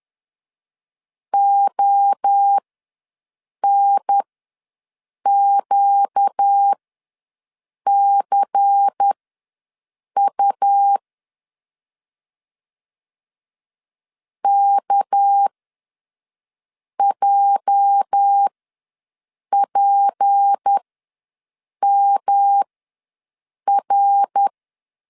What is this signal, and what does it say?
Morse code, heard as a keyed tone, about 10 wpm, spaced as for 6 wpm, 800 Hz. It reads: ONQCU KJPMR
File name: LEZIONI / 8 YJ45 / 8 YJ ws 10wpm 4043